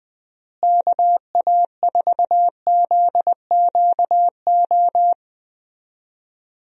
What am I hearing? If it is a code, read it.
Morse code, heard as a keyed tone, about 21 wpm, 700 Hz. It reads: KA4ZQO
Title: USA Call Signs - 6 Characters - Repeated 3x - 20wpm